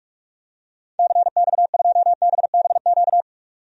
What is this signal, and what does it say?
Morse code, heard as a keyed tone, about 46 wpm, 700 Hz. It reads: CX2BBQ
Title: DXCC-100 Call Signs - 6 Characters - 45wpm